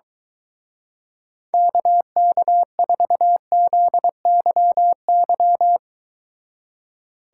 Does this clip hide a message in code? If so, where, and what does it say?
Yes: Morse code from 1.5 to 5.8 s, heard as a keyed tone, about 23 wpm, 700 Hz. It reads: KK4ZYY